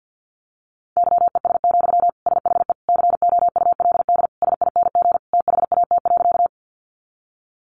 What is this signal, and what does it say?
Morse code, heard as a keyed tone, about 50 wpm, 700 Hz. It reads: YES, SHE COULD SING THAT.
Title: Sentences from Top 500 Words 50wpm